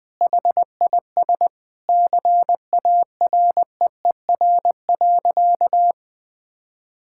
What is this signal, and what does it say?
Morse code, heard as a keyed tone, about 20 wpm, 700 Hz. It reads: HIS CAREER.
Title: Sentences with 7-Word Phrases - 20wpm